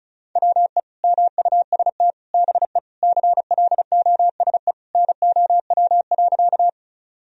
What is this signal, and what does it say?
Morse code, heard as a keyed tone, about 36 wpm, 700 Hz. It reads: WE MUST BE CLOSE NOW.
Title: Sentences from Top 1000 Words - Review Part 1 - 35wpm